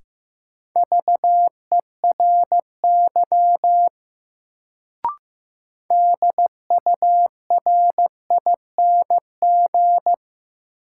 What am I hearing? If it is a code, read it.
Morse code, heard as a keyed tone, about 15 wpm, 700 Hz. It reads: VERY DURING
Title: Sets of 2 Words - New Word Drill - 15wpm